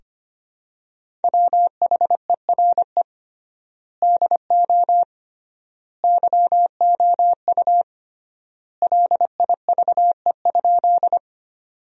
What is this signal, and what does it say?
Morse code, heard as a keyed tone, about 25 wpm, 700 Hz. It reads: WHERE DO YOU LIVE?
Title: 3x Word-Spacing - Sentences from Top 200 Words - 25wpm